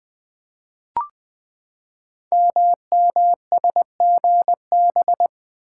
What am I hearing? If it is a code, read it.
Morse code, heard as a keyed tone, about 20 wpm, 700 Hz. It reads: MMSGB
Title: Code Groups - 5 Letters - 20wpm